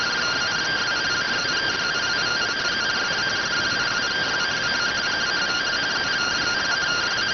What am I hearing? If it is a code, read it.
Two-tone teleprinter signal, about 170 Hz shift.